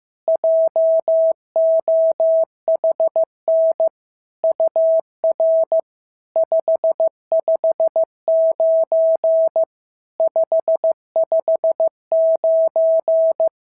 Morse code, heard as a keyed tone, about 15 wpm, 650 Hz. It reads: JOHN UR 559 559
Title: SOTA QSO Practice 15wpm